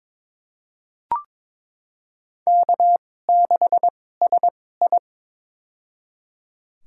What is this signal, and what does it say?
Morse code, heard as a keyed tone, about 21 wpm, spaced as for 16 wpm, 700 Hz. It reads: K6SI
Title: USA Call Signs - 2x Character Spacing - 22wpm